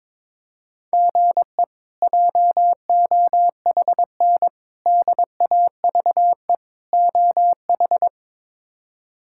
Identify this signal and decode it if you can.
Morse code, heard as a keyed tone, about 22 wpm, 700 Hz. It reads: GE JOHN DAVE OH
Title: K1USN SST QSO Practice - 22wpm